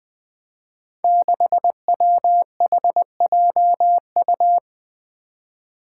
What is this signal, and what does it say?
Morse code, heard as a keyed tone, about 20 wpm, 700 Hz. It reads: 6WHJU
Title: Five Challenging Letter-Number Combinations 20wpm